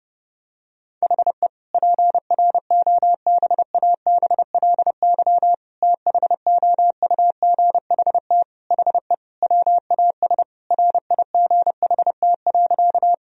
Morse code, heard as a keyed tone, about 32 wpm, 700 Hz. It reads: HE PROBABLY THOUGHT HE WAS RIGHT.
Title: Sentences from Top 1000 Words - Review Part 4 - 30wpm